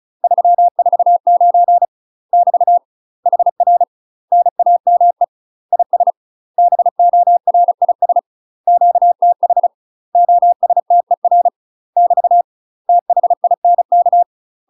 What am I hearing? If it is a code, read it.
Morse code, heard as a keyed tone, about 34 wpm, 700 Hz. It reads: 349 <BT> HR NAME IS BORIS QTH OSTER <BT> THINK
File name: UK MASTER QSOs TRAINING / 35WPM / UK MASTER QSOs 0024